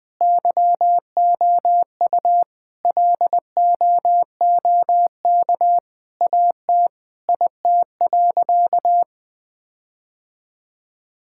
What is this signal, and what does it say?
Morse code, heard as a keyed tone, about 21 wpm, 700 Hz. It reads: YOU LOOK AT IT.
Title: Sentences with 7-Word Phrases - 20wpm